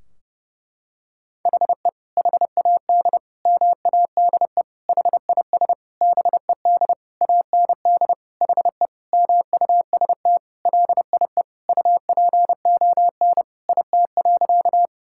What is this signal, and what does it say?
Morse code, heard as a keyed tone, about 30 wpm, 700 Hz. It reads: HE HAD MADE HIS BED AND HE MUST LIE UPON IT.
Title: Sentences from Top 1600 Words 30wpm